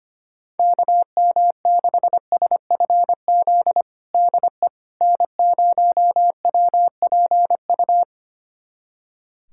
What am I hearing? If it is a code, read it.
Morse code, heard as a keyed tone, about 25 wpm, 700 Hz. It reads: KM6SFZ DE N0WPU
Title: USA Callsign DE Callsign 25wpm